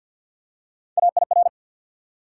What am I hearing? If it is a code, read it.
Morse code, heard as a keyed tone, about 54 wpm, 700 Hz. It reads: AIR
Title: Three Letter Words - 50wpm